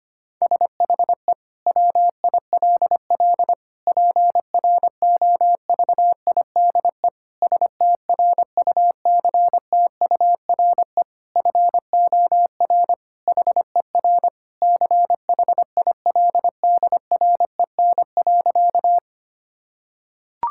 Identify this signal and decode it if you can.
Morse code, heard as a keyed tone, about 25 wpm, 700 Hz. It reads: SHE WILL PROVIDE STRUCTURE FOR HER CHILDREN.
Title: Sentences from Top 1000 Words - Review Part 6 - 25wpm